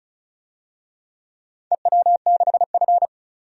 Morse code, heard as a keyed tone, about 36 wpm, 700 Hz. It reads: EW6F
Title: DXCC-100 Call Signs - 4 Characters - 35wpm